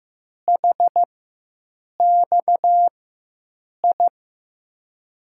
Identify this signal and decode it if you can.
Morse code, heard as a keyed tone, about 15 wpm, spaced as for 7 wpm, 700 Hz. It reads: HXI
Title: USA Call Signs - 6 Characters - 4x Character Spacing - 15wpm